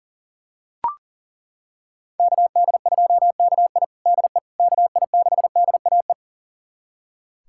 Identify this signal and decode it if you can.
Morse code, heard as a keyed tone, about 40 wpm, 700 Hz. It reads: KD2KI DE KI6DAE
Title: USA Callsign DE Callsign 40wpm